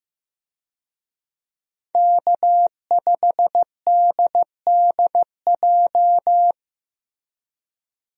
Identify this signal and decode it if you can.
Morse code, heard as a keyed tone, about 16 wpm, 700 Hz. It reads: K5DDJ
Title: USA Call Signs - 5 Characters - Repeated 2x - 15wpm